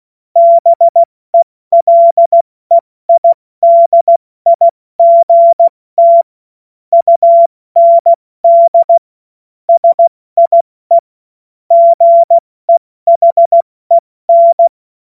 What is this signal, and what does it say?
Morse code, heard as a keyed tone, about 16 wpm, 680 Hz. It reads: BELEIDIGT UND SIE GEHEN
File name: Morsekurs / Witze / Witz 07.November 2023 70 F80